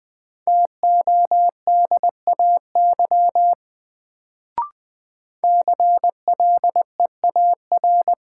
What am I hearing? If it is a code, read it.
Morse code, heard as a keyed tone, about 20 wpm, 700 Hz. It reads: TODAY CLEAR
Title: Sets of 5 Words Encore - New Word Drill - 20wpm